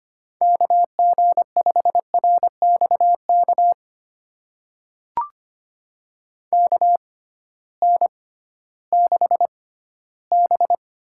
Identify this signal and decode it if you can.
Morse code, heard as a keyed tone, about 25 wpm, spaced as for 14 wpm, 700 Hz. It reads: KG5RXK KN6B
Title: USA Call Signs - 6x Character Spacing - 25wpm